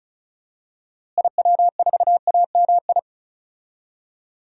Morse code, heard as a keyed tone, about 36 wpm, 700 Hz. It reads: IW4AMI
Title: DXCC-100 Call Signs - 6 Characters - Repeated 3x - 35wpm